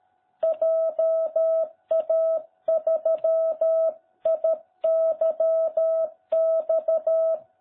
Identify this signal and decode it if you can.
Morse code, heard as a keyed tone, about 13 wpm, 660 Hz. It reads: JA3IYX